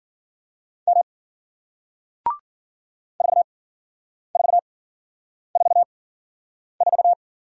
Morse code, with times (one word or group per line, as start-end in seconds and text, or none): 0.9-1.0 M
3.2-7.1 4444